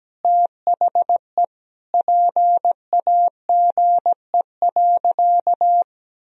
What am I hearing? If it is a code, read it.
Morse code, heard as a keyed tone, about 17 wpm, 700 Hz. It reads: THE PAGE.
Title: Sentences from Top 1000 Words - Review Part 1 - 17wpm